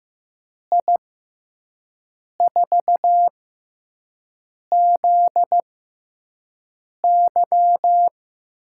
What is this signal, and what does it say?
Morse code, heard as a keyed tone, about 15 wpm, spaced as for 5 wpm, 700 Hz. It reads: I4ZY
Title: USA Call Signs - 6 Characters - 6x Character Spacing - 15wpm